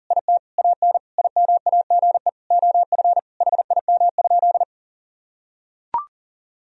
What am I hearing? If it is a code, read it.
Morse code, heard as a keyed tone, about 40 wpm, 700 Hz. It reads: IT AN IMAGE OF HIM?
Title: Sentences from Top 1000 Words - Review Part 4 - 40wpm